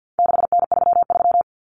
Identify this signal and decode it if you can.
Morse code, heard as a keyed tone, about 50 wpm, 700 Hz. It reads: 6N33
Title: Common Cut Numbers - 4-Digit 50wpm